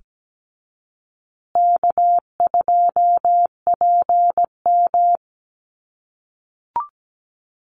Morse code, heard as a keyed tone, about 17 wpm, 700 Hz. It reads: K2PM
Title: USA Call Signs Repeated 3x 17wpm